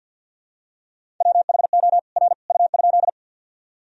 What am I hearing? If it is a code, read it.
Morse code, heard as a keyed tone, about 50 wpm, 700 Hz. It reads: WHO R U?
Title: Sentences from Top 100 Words and Abbreviations 50wpm